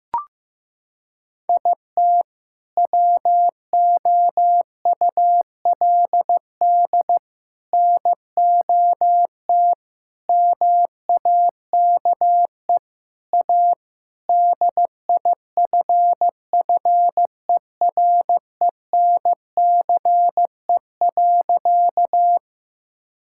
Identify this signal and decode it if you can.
Morse code, heard as a keyed tone, about 15 wpm, 700 Hz. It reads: IT WOULD NOT MAKE A DIFFERENCE.
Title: Sentences from Top 1000 Words - Review Part 5 - 15wpm